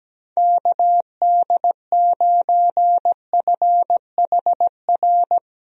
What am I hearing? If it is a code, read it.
Morse code, heard as a keyed tone, about 17 wpm, 700 Hz. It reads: KD9FHR
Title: USA Call Signs - 6 Characters - Repeated 2x - 17wpm